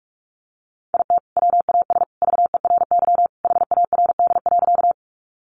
Morse code, heard as a keyed tone, about 44 wpm, 700 Hz. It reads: IT WAS VERY HARD.
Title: Sentences from Top 300 Words 45wpm